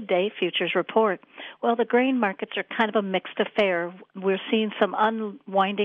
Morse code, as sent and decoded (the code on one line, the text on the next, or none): none